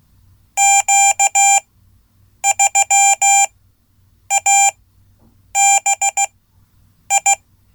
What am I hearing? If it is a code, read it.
Morse code, heard as a keyed tone, about 16 wpm, spaced as for 8 wpm, 790 Hz. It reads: Q3ABI
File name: Code / 5min-a